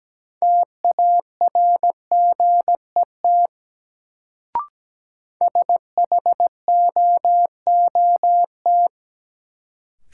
Morse code, heard as a keyed tone, about 17 wpm, 700 Hz. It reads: TARGET SHOOT
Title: Sentences from Top 1200 Words - New Word Drill - 17wpm